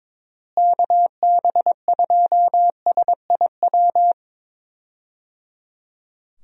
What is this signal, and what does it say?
Morse code, heard as a keyed tone, about 22 wpm, 700 Hz. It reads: KB2SIW
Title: USA Call Signs - 6 Characters - Repeated 3x - 22wpm